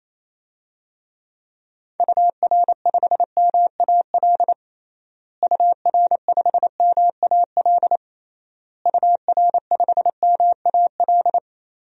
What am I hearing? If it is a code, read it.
Morse code, heard as a keyed tone, about 28 wpm, 700 Hz. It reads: UR5MAL UR5MAL UR5MAL
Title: DXCC-100 Call Signs - 6 Characters - Repeated 3x - 28wpm